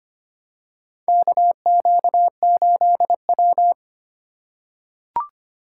Morse code, heard as a keyed tone, about 25 wpm, 700 Hz. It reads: KQ8W
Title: USA Call Signs - 4 Characters - 2x Character Spacing - 25wpm